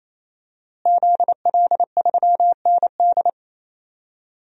Morse code, heard as a keyed tone, about 28 wpm, 700 Hz. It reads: ZL3ND
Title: DXCC-100 Call Signs - 5 Characters - Repeated 3x - 28wpm